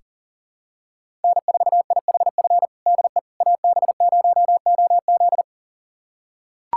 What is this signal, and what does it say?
Morse code, heard as a keyed tone, about 37 wpm, 700 Hz. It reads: N4IHF DE AB0OZ
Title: USA Callsign DE Callsign - 5 Characters - 40wpm